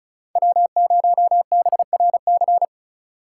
Morse code, heard as a keyed tone, about 37 wpm, 700 Hz. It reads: W0BRC
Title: Speed Racing - USA 1x3 Call Signs 30wpm